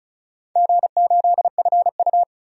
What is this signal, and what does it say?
Morse code, heard as a keyed tone, about 36 wpm, 700 Hz. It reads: G8FU
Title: DXCC-100 Call Signs - 4 Characters - 35wpm